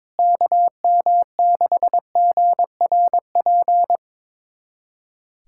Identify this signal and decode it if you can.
Morse code, heard as a keyed tone, about 22 wpm, 700 Hz. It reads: KM6GRP
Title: USA Callsign DE Callsign - 6 Characters - 22wpm